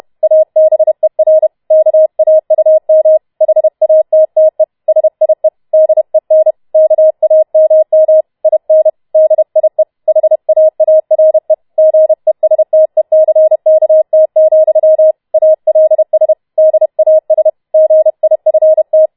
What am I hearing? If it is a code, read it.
Morse code, heard as a keyed tone, about 31 wpm, 630 Hz. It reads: ABER KAUM HATTE SIE DEN KAMM IN DIE HAARE GESTECKT, ALS DAS GIFT
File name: Schneewittchen Teil4 32wpm